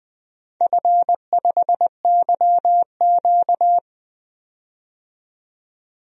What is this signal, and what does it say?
Morse code, heard as a keyed tone, about 20 wpm, 700 Hz. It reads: F5YQ
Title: DXCC-100 Call Signs - 4 Characters - 20wpm